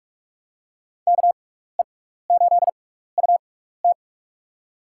Morse code, heard as a keyed tone, about 46 wpm, spaced as for 16 wpm, 700 Hz. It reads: KE8UT
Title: USA Call Signs - 5 Characters - 6x Character Spacing - 45wpm